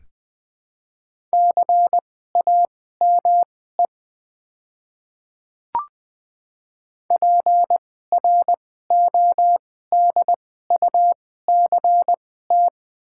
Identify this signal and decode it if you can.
Morse code, heard as a keyed tone, about 20 wpm, spaced as for 14 wpm, 700 Hz. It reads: CAME PRODUCT
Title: 2x Character Spacing - Top 500 Words - 20wpm